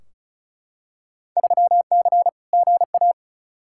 Morse code, heard as a keyed tone, about 36 wpm, 700 Hz. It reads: 3C GA
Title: Field Day Exchanges Encore - 35wpm